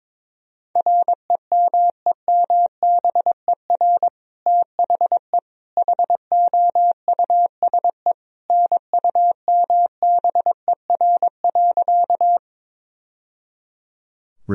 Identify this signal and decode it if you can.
Morse code, heard as a keyed tone, about 22 wpm, 700 Hz. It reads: REMEMBER THE HOUSE NUMBER.
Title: Sentences from Top 1000 Words - Review Part 1 - 22wpm